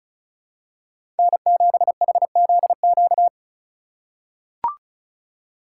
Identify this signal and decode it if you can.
Morse code, heard as a keyed tone, about 36 wpm, 700 Hz. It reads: N7HZQ
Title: USA Call Signs - 5 Characters - 2x Character Spacing - 35wpm